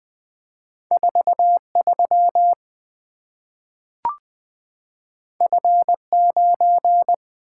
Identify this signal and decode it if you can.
Morse code, heard as a keyed tone, about 20 wpm, 700 Hz. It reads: 4 3 F 9
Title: Two Letter-Number Combinations - 20wpm